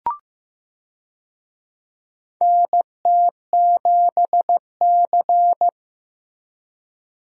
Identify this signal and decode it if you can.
Morse code, heard as a keyed tone, about 15 wpm, 700 Hz. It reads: NT7C
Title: USA Call Signs - 4 Characters - Repeated 3x - 15wpm